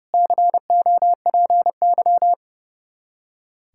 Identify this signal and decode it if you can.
Morse code, heard as a keyed tone, about 37 wpm, 700 Hz. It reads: COPY
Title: Speed-Racing Common QSO Phrases - 30wpm